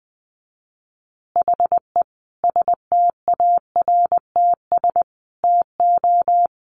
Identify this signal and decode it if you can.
Morse code, heard as a keyed tone, about 20 wpm, 700 Hz. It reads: HE STARTS TO
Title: Additional Easy Phrases 20wpm